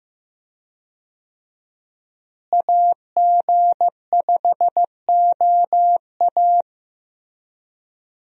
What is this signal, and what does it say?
Morse code, heard as a keyed tone, about 15 wpm, 700 Hz. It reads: AG5OA
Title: USA Call Signs - 5 Characters - Repeated 2x - 15wpm